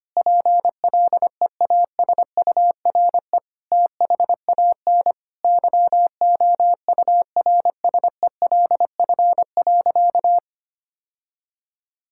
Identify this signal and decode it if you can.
Morse code, heard as a keyed tone, about 25 wpm, 700 Hz. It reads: PLEASURE THAN YOURSELF.
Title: Sentences from Top 1600 Words 25wpm